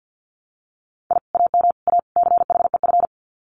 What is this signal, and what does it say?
Morse code, heard as a keyed tone, about 50 wpm, 700 Hz. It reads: I AM A CHEF
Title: Common QSO Phrases Encore 50wpm